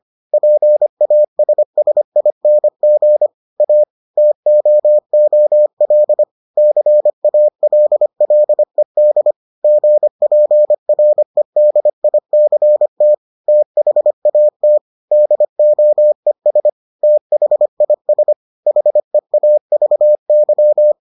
Morse code, heard as a keyed tone, about 25 wpm, 600 Hz. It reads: PASSING A TOOL CALLED GPREDICT THAT DOES THIS HEAVY